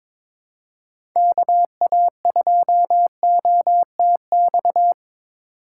Morse code, heard as a keyed tone, about 22 wpm, 700 Hz. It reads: KA2OTX
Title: USA Call Signs - 2x Character Spacing - 22wpm